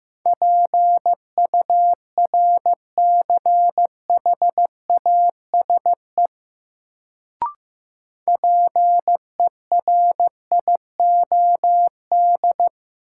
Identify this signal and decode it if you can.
Morse code, heard as a keyed tone, about 15 wpm, 700 Hz. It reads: PURCHASE PERIOD
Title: Sentences from Top 800 Words - New Word Drill - 15wpm